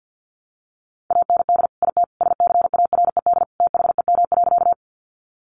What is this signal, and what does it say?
Morse code, heard as a keyed tone, about 49 wpm, 700 Hz. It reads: AND IT SCARED THEM.
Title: Sentences from Top 700 Words 50wpm